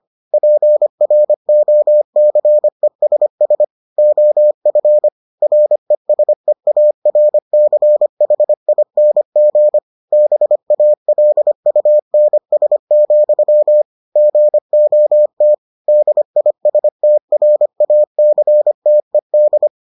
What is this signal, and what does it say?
Morse code, heard as a keyed tone, about 25 wpm, 600 Hz. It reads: PROCESS OF RESEARCHING BALUNS, GOT DISTRACTED